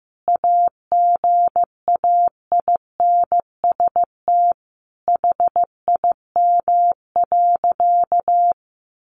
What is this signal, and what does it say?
Morse code, heard as a keyed tone, about 15 wpm, 700 Hz. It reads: AGAINST HIM.
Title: Sentences from Top 1300 Words 15wpm